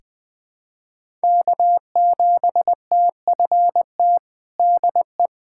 Morse code, heard as a keyed tone, about 21 wpm, 700 Hz. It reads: K7TFT DE
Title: USA Callsign DE Callsign - 5 Characters - 20wpm